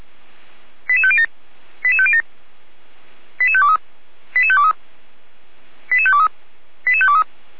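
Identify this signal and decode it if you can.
Five-tone selective call (ZVEI), heard as quick runs of five tones, tones of about 70 ms. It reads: 88507 88507 88523 88523 88523 88523
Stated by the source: ZVEI five-tone call received by SDR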